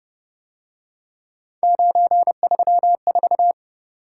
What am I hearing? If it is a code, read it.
Morse code, heard as a keyed tone, about 30 wpm, 700 Hz. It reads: 934
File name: Speed-Racing - Three Letter-Number Combinations - 25wpm